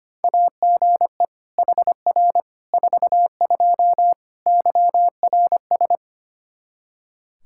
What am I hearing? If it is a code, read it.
Morse code, heard as a keyed tone, about 25 wpm, 700 Hz. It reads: AGE HR 42 YRS
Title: Practice QSO - Exchange 3 - More Info - 25wpm